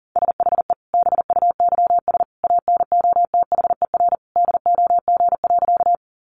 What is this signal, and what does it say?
Morse code, heard as a keyed tone, about 40 wpm, 700 Hz. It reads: SHE BUYS ANOTHER DOG.
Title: Easy Phrases 40wpm